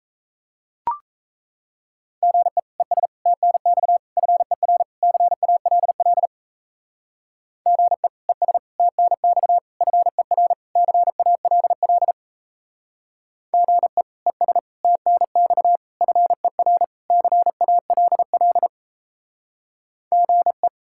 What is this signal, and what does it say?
Morse code, heard as a keyed tone, about 35 wpm, 700 Hz. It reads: GE ES TNX FER CALL GE ES TNX FER CALL GE ES TNX FER CALL GE
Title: Speed-Racing Common QSO Phrases - 28wpm